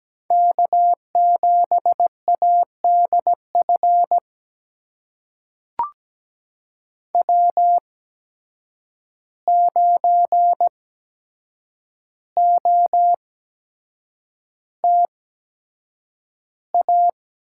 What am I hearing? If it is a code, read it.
Morse code, heard as a keyed tone, about 17 wpm, spaced as for 7 wpm, 700 Hz. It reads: K7ADF W9OTA
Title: USA Call Signs - 5 Characters - 8x Character Spacing - 17wpm